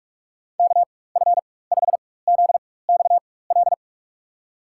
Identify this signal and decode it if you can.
Morse code, heard as a keyed tone, about 47 wpm, spaced as for 21 wpm, 700 Hz. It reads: KF5ZXL
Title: USA Call Signs - 6 Characters - 4x Character Spacing - 45wpm